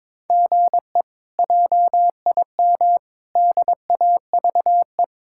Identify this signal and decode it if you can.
Morse code, heard as a keyed tone, about 22 wpm, 700 Hz. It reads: GE JIM DAVE
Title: K1USN SST QSO Practice - 22wpm